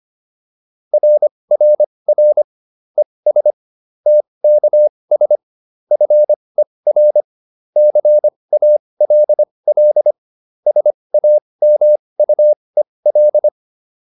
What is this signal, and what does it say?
Morse code, heard as a keyed tone, about 25 wpm, spaced as for 20 wpm, 600 Hz. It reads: RRR ES TKS FER CALL SAMUEL